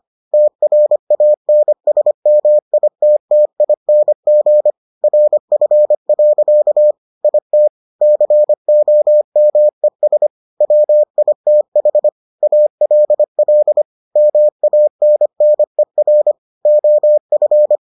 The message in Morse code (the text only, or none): TRANSMITTING RF. IT COMES WITH ALL MANNER OF